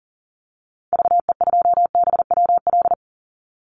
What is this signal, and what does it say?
Morse code, heard as a keyed tone, about 38 wpm, 700 Hz. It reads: VE2BWL
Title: Call Signs - 40wpm